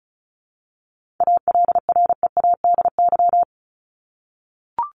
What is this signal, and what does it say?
Morse code, heard as a keyed tone, about 34 wpm, 700 Hz. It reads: ALREADY
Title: Top 501-750 Words 35wpm